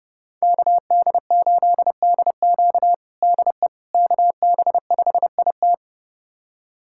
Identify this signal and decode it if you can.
Morse code, heard as a keyed tone, about 30 wpm, 700 Hz. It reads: KD8DQ DE KB5IT
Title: USA Callsign DE Callsign - 5 Characters - 30wpm